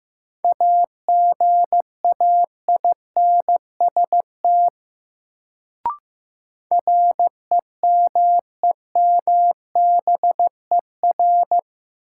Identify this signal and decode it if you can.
Morse code, heard as a keyed tone, about 15 wpm, 700 Hz. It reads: AGAINST REMEMBER
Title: Sentences from Top 400 Words - New Word Drill - 15wpm